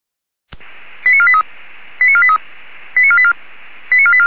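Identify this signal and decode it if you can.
Five-tone selective call (ZVEI), heard as quick runs of five tones, tones of about 70 ms. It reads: 89483 89483 89484 89484
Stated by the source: ZVEI five-tone selective call received on an SDR.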